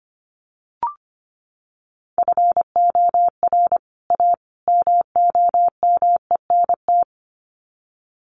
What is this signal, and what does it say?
Morse code, heard as a keyed tone, about 25 wpm, 700 Hz. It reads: FOR A MOMENT
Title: Sets of 3 Words 25wpm